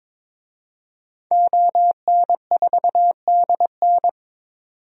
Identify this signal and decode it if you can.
Morse code, heard as a keyed tone, about 22 wpm, 700 Hz. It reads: ON4DN
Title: DXCC-100 Call Signs - 5 Characters - 22wpm